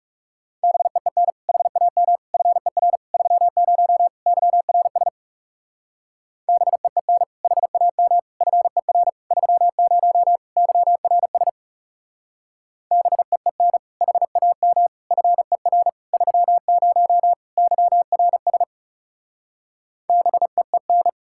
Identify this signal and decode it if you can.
Morse code, heard as a keyed tone, about 37 wpm, 700 Hz. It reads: BEEN HAM FER 30 YRS BEEN HAM FER 30 YRS BEEN HAM FER 30 YRS BEEN
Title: Speed-Racing Common QSO Phrases - 30wpm